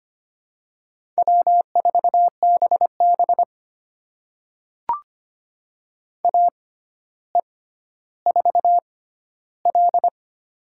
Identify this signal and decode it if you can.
Morse code, heard as a keyed tone, about 25 wpm, spaced as for 13 wpm, 700 Hz. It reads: W4BB AE4L
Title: USA Call Signs - 4 Characters - 6x Character Spacing - 25wpm